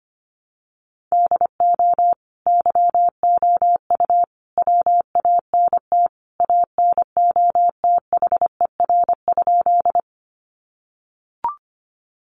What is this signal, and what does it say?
Morse code, heard as a keyed tone, about 25 wpm, 700 Hz. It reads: DO YOU WANT ANOTHER?
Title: Sentences from Top 600 Words 25wpm